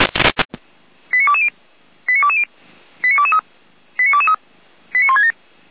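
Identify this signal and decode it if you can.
Five-tone selective call (ZVEI), heard as quick runs of five tones, tones of about 70 ms. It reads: 89220 89220 89223 89223 89167
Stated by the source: ZVEI five-tone selective call 89022 received on SDR, from Feuerwehr Orlamünde audio archive.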